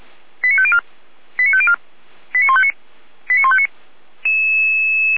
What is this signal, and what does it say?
Five-tone selective call (ZVEI), heard as quick runs of five tones, tones of about 70 ms. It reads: 89504 89504 89159 89159
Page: SDR/ZVEI